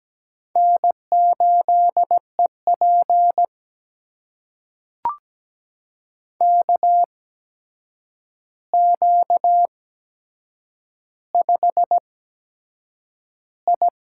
Morse code, heard as a keyed tone, about 17 wpm, spaced as for 7 wpm, 700 Hz. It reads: N8EP KQ5I
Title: USA Call Signs - 4 Characters - 8x Character Spacing - 17wpm